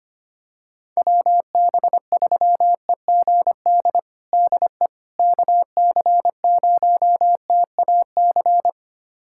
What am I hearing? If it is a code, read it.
Morse code, heard as a keyed tone, about 25 wpm, 700 Hz. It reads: WB3EGD DE KC0TAC
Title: USA Callsign DE Callsign - 6 Characters - 25wpm